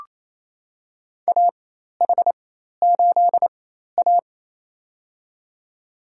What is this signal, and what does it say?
Morse code, heard as a keyed tone, about 28 wpm, spaced as for 13 wpm, 700 Hz. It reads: AH8A
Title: USA Call Signs - 4 Characters - 4x Character Spacing - 28wpm